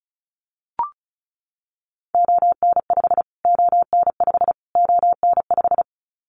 Morse code, heard as a keyed tone, about 34 wpm, 700 Hz. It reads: ON5 ON5 ON5
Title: Callsign Prefixes for Top 25 DXCC - Repeated 3x - 35wpm